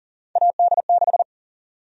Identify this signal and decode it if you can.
Morse code, heard as a keyed tone, about 40 wpm, 700 Hz. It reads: AD6
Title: USA Call Sign Prefixes - 40wpm